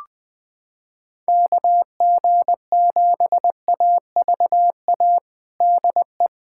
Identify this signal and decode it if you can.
Morse code, heard as a keyed tone, about 21 wpm, 700 Hz. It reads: KG7AVA DE
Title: USA Callsign DE Callsign - 6 Characters - 20wpm